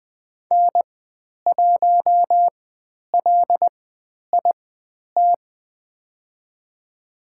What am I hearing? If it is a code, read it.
Morse code, heard as a keyed tone, about 20 wpm, spaced as for 10 wpm, 700 Hz. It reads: N1LIT
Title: 095 Lesson 16g - Letter c - Call Signs 20wpm